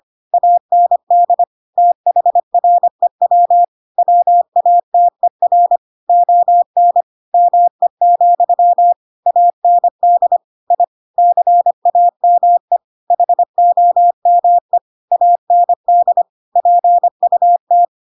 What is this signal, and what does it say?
Morse code, heard as a keyed tone, about 25 wpm, 700 Hz. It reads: AND THREW WATER ON ME, AND I CAME HOME AND PUT
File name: Pinocchio / Pinocchio ch7 25 25 ik1hge